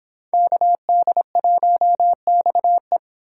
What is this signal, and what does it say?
Morse code, heard as a keyed tone, about 26 wpm, 700 Hz. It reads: KD1XE